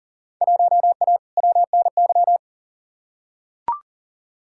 Morse code, heard as a keyed tone, about 37 wpm, 700 Hz. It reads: 1A WNY
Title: Field Day Exchanges Encore - 40wpm